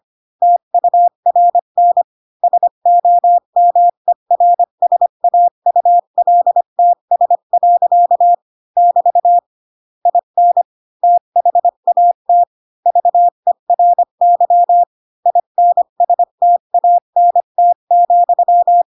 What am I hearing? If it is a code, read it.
Morse code, heard as a keyed tone, about 25 wpm, 700 Hz. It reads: TURN SOMERSAULTS. <BT> IN THAT VERY INSTANT,
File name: Pinocchio / Pinocchio ch2 20 25 ik1hge